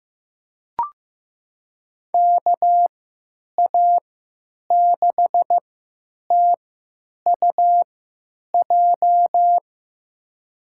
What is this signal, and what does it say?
Morse code, heard as a keyed tone, about 15 wpm, spaced as for 9 wpm, 700 Hz. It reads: KA6TUJ
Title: USA Call Signs - 6 Characters - 3x Character Spacing - 15wpm